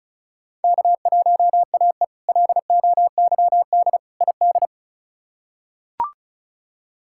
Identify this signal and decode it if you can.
Morse code, heard as a keyed tone, about 34 wpm, 700 Hz. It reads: K1AE LOYD ID
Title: K1USN SST QSO Practice - 35wpm